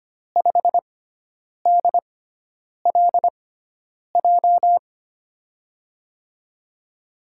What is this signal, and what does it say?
Morse code, heard as a keyed tone, about 25 wpm, spaced as for 9 wpm, 700 Hz. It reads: 5DLJ
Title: USA Call Signs - 6x Character Spacing - 25wpm